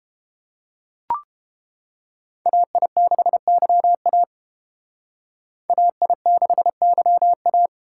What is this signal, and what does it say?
Morse code, heard as a keyed tone, about 31 wpm, 700 Hz. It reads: AI6YA AI6YA